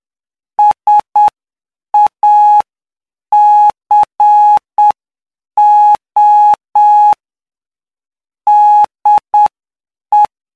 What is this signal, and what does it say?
Morse code, heard as a keyed tone, about 7 wpm, 840 Hz. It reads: SACO DE
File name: cw / py2lcd / 16-PY2LCD-Li��o 25